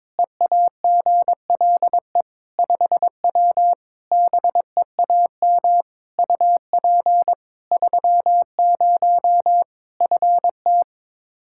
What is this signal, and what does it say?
Morse code, heard as a keyed tone, about 21 wpm, 700 Hz. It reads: EAGLE 5W BEAM UP 30 FT
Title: Practice QSO - Exchange 2 - Info - 22wpm